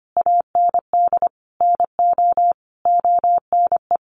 Morse code, heard as a keyed tone, about 25 wpm, 700 Hz. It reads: AND NO ONE
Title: Sentences from Top 1400 Words 25wpm